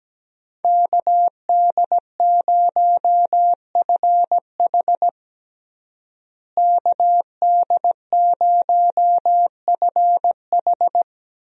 Morse code, heard as a keyed tone, about 17 wpm, 700 Hz. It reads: KD0FH KD0FH
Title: USA Call Signs - 5 Characters - Repeated 2x - 17wpm